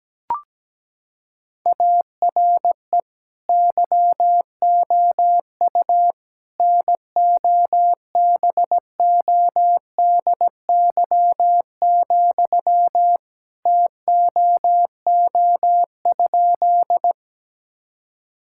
Morse code, heard as a keyed tone, about 17 wpm, 700 Hz. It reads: ARE YOU NOBODY, TOO?
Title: Sentences from Top 1600 Words 17wpm